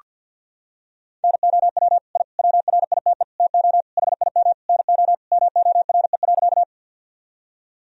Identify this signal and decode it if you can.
Morse code, heard as a keyed tone, about 49 wpm, 700 Hz. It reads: NOW I WRITE TO HIM NO MORE.
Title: Sentences from Top 1000 Words - Review Part 1 - 50wpm